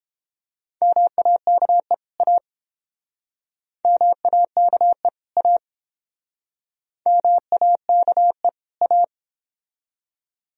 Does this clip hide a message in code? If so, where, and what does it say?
Yes: Morse code from 0.8 to 9.0 s, heard as a keyed tone, about 30 wpm, 700 Hz. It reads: MAKE A MAKE A MAKE A